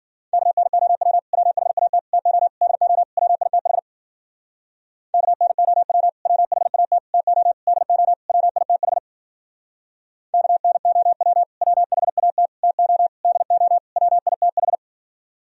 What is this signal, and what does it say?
Morse code, heard as a keyed tone, about 52 wpm, 700 Hz. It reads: KNOW WHAT TO DO WITH KNOW WHAT TO DO WITH KNOW WHAT TO DO WITH